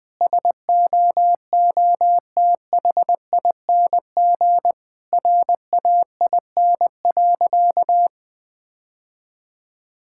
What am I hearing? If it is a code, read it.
Morse code, heard as a keyed tone, about 20 wpm, 700 Hz. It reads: SOOTHING RAIN.
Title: Difficult Sentences with Hints - 20wpm